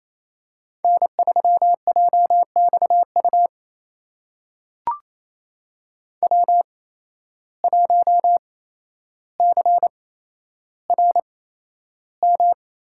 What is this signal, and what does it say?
Morse code, heard as a keyed tone, about 28 wpm, spaced as for 12 wpm, 700 Hz. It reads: N3JXU W1CRM
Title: USA Call Signs - 5 Characters - 8x Character Spacing - 28wpm